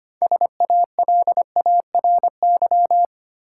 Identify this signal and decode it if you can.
Morse code, heard as a keyed tone, about 25 wpm, 700 Hz. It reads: SALARY
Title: Six Letter Words - 25wpm